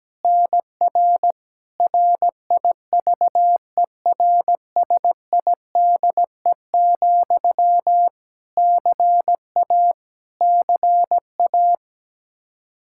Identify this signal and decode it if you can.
Morse code, heard as a keyed tone, about 17 wpm, 700 Hz. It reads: NR RIVERSIDE, CA CA
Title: Practice QSO - Exchange 1 - Beginning - 17wpm